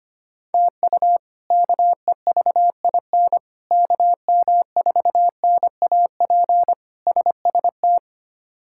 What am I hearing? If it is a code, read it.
Morse code, heard as a keyed tone, about 25 wpm, 700 Hz. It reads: TU KEVIN KM4NAP SST
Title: K1USN SST QSO Practice - 25wpm